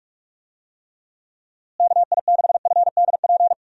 Morse code, heard as a keyed tone, about 47 wpm, 700 Hz. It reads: KI6FDP
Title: USA Call Signs - 6 Characters - Repeated 3x - 45wpm